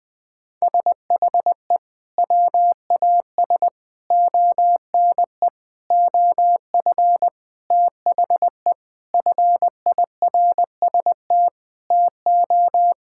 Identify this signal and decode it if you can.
Morse code, heard as a keyed tone, about 20 wpm, 700 Hz. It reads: SHE WAS ONE OF THE FIRST TO
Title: Sentences with 4-Word Phrases - 20wpm